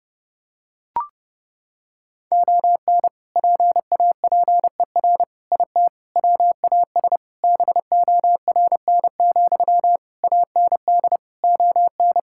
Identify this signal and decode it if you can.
Morse code, heard as a keyed tone, about 30 wpm, 700 Hz. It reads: ON PAPER IT WAS BORN, AND ON
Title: Sentences from Top 1600 Words 30wpm